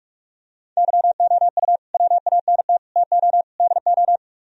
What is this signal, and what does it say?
Morse code, heard as a keyed tone, about 50 wpm, 690 Hz. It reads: YOU WANT TO DO